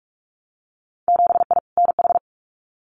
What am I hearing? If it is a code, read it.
Morse code, heard as a keyed tone, about 44 wpm, 700 Hz. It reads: 7I NH
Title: Winter Field Day Exchanges Encore 45wpm